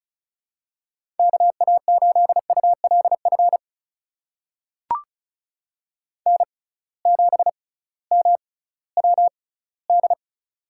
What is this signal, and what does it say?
Morse code, heard as a keyed tone, about 34 wpm, spaced as for 19 wpm, 700 Hz. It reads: KA8ULF N7MWD
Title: USA Call Signs - 6x Character Spacing - 35wpm